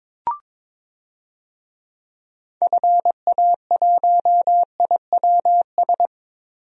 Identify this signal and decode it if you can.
Morse code, heard as a keyed tone, about 22 wpm, 700 Hz. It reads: FA1IWS
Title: DXCC-100 Call Signs - 6 Characters - Repeated 3x - 22wpm